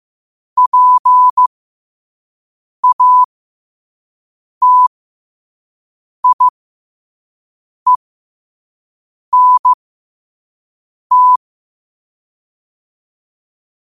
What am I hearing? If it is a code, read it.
Morse code, heard as a keyed tone, about 15 wpm, spaced as for 5 wpm, 1000 Hz. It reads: PATIENT
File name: qotdfiles / 5wpm